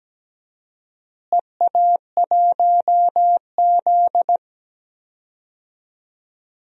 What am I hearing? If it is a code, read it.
Morse code, heard as a keyed tone, about 17 wpm, 700 Hz. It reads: EA1Z
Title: DXCC-100 Call Signs - 4 Characters - Repeated 3x - 17wpm